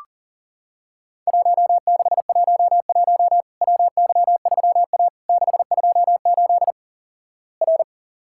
Morse code, heard as a keyed tone, about 44 wpm, 700 Hz. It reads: 1611 WY3A 628 R
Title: Worked All Europe DX Contest - QTC Exchange - 40wpm